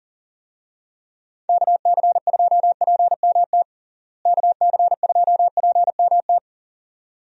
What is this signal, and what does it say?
Morse code, heard as a keyed tone, about 40 wpm, 700 Hz. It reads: KC2PMT KC2PMT
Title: USA Call Signs - 6 Characters - Repeated 2x - 40wpm